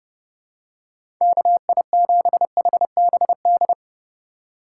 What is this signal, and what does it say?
Morse code, heard as a keyed tone, about 30 wpm, 700 Hz. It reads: KI7HBD
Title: USA Call Signs - 6 Characters - 2x Character Spacing - 30wpm